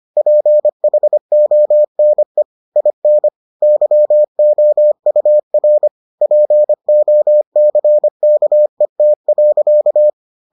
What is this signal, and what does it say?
Morse code, heard as a keyed tone, about 25 wpm, 600 Hz. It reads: PHONE IN YOUR POCKET.